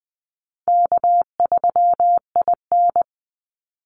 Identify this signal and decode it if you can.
Morse code, heard as a keyed tone, about 20 wpm, 700 Hz. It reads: K3IN
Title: Call Signs Repeated 3x - 20wpm